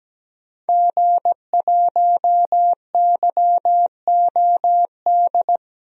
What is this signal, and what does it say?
Morse code, heard as a keyed tone, about 17 wpm, 700 Hz. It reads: G1YOD
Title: DXCC-100 Call Signs - 5 Characters - 17wpm